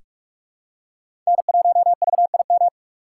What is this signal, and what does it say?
Morse code, heard as a keyed tone, about 47 wpm, 700 Hz. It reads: N1VIM
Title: USA Call Signs - 5 Characters - 3x Character Spacing - 45wpm